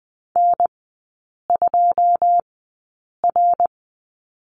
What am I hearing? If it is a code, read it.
Morse code, heard as a keyed tone, about 20 wpm, spaced as for 8 wpm, 700 Hz. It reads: N2R
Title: 172 Lesson 28e - Number 9 - Three Characters 20wpm